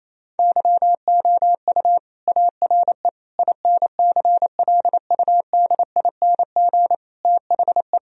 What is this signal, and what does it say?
Morse code, heard as a keyed tone, about 28 wpm, 700 Hz. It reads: YOU ARE INCLUDING THE
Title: Sentences from Top 1000 Words - Review Part 4 - 28wpm